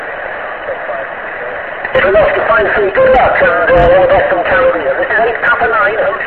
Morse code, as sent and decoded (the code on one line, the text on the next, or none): none